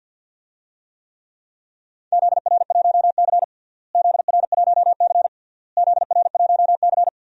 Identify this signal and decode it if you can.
Morse code, heard as a keyed tone, about 45 wpm, 700 Hz. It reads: ZR1C ZR1C ZR1C
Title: DXCC-100 Call Signs - 4 Characters - Repeated 3x - 50wpm